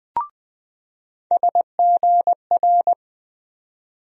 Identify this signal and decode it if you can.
Morse code, heard as a keyed tone, about 21 wpm, 700 Hz. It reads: SGR
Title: USA Call Sign Postfixes - 3 Letters 20wpm